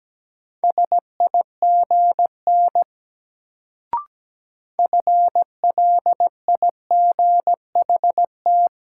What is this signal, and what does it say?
Morse code, heard as a keyed tone, about 17 wpm, 700 Hz. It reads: SIGN FLIGHT
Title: Sentences from Top 1000 Words - New Word Drill - 17wpm